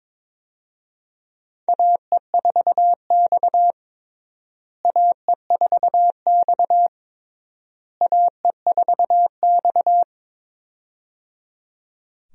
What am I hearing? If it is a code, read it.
Morse code, heard as a keyed tone, about 21 wpm, 700 Hz. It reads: AE4X AE4X AE4X
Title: USA Call Signs - 4 Characters - Repeated 3x - 22wpm